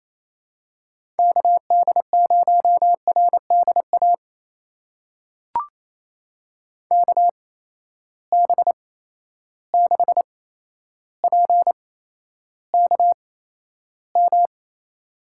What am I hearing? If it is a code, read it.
Morse code, heard as a keyed tone, about 27 wpm, spaced as for 12 wpm, 700 Hz. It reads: KD0RDA KB6PKM
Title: USA Call Signs - 8x Character Spacing - 28wpm